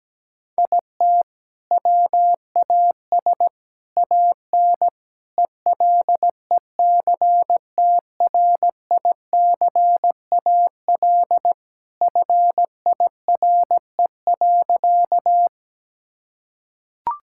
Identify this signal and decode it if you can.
Morse code, heard as a keyed tone, about 17 wpm, 700 Hz. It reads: IT WAS AN ELECTRICAL FIRE.
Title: Sentences from Top 1000 Words - Review Part 6 - 17wpm